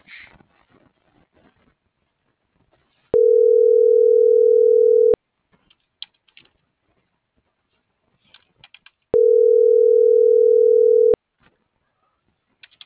Ringback tone (North America), 440 + 480 Hz.